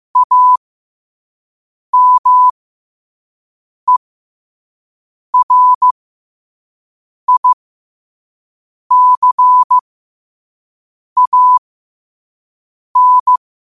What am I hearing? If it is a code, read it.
Morse code, heard as a keyed tone, about 15 wpm, spaced as for 5 wpm, 1000 Hz. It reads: AMERICAN
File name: qotdfiles / 5wpm